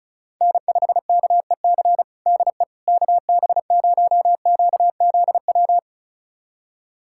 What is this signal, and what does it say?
Morse code, heard as a keyed tone, about 36 wpm, 700 Hz. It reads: N5KEC DE KB0QZW
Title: USA Callsign DE Callsign 35wpm